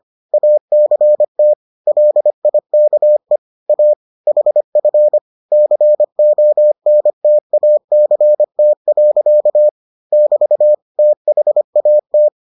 Morse code, heard as a keyed tone, about 25 wpm, 600 Hz. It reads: ACT LIKE A HF CONTACT. <BT> THAT